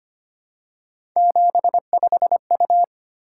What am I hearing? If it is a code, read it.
Morse code, heard as a keyed tone, about 25 wpm, 700 Hz. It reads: 75U
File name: Three Letter-Number Combinations - 25wpm